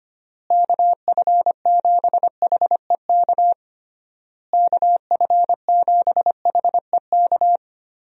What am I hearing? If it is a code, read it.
Morse code, heard as a keyed tone, about 25 wpm, 700 Hz. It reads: KF7HEK KF7HEK
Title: USA Call Signs - 6 Characters - Repeated 3x - 25wpm